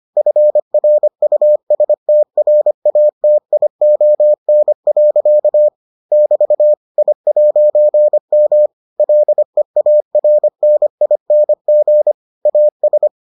Morse code, heard as a keyed tone, about 25 wpm, 600 Hz. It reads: FRUSTRATION. <BT> I'M LEARNING AS